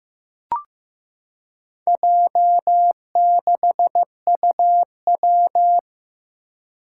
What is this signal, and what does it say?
Morse code, heard as a keyed tone, about 15 wpm, 700 Hz. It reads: J6UW
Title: Four Challenging Letter-Number Combinations 15wpm